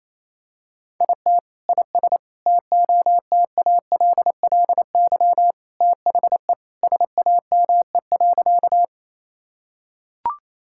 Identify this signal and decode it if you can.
Morse code, heard as a keyed tone, about 28 wpm, 700 Hz. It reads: IT IS TOTALLY THE SAME.
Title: Sentences from Top 1600 Words 28wpm